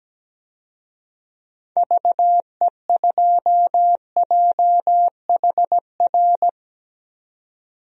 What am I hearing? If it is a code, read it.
Morse code, heard as a keyed tone, about 17 wpm, 700 Hz. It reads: VE2JHR
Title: DXCC-100 Call Signs - 6 Characters - 17wpm